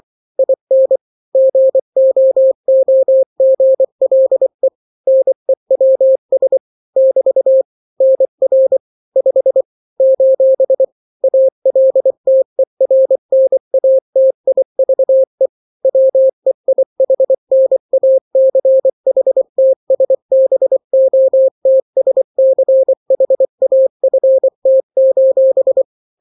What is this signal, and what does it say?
Morse code, heard as a keyed tone, about 24 wpm, 530 Hz. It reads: IN GOOGLE NEWS <BT> NR 5 : ALTERNATIVE WEIHNACHTSBOTSCHAFT:
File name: daily / 23wpm